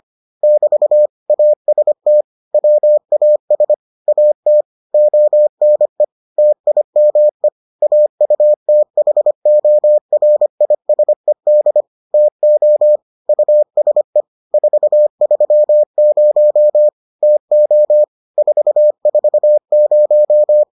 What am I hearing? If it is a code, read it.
Morse code, heard as a keyed tone, about 25 wpm, 600 Hz. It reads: <BT> AST WAS AT ONE TIME AUTHORISED TO USE 430 TO 440